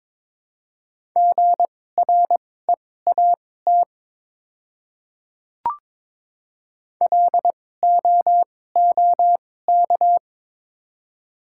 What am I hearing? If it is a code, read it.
Morse code, heard as a keyed tone, about 23 wpm, spaced as for 16 wpm, 700 Hz. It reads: GREAT LOOK
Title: 2x Character Spacing - Top 500 Words - 22wpm